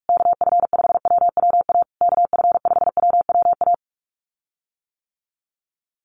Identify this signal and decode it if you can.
Morse code, heard as a keyed tone, about 44 wpm, 700 Hz. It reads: KF5WWA KF5WWA
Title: USA Call Signs Repeated 3x 45wpm